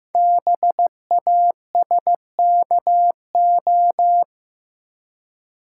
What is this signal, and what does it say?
Morse code, heard as a keyed tone, about 15 wpm, 700 Hz. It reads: BASKO
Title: Code Groups - 5 Letters - 15wpm